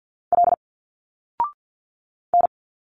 Morse code, heard as a keyed tone, about 47 wpm, 700 Hz. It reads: L N
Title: Single Letters 50wpm